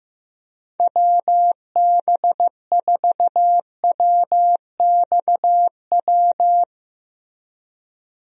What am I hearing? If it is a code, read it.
Morse code, heard as a keyed tone, about 15 wpm, 700 Hz. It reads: WB4WXW
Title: USA Call Signs - 6 Characters - Repeated 2x - 15wpm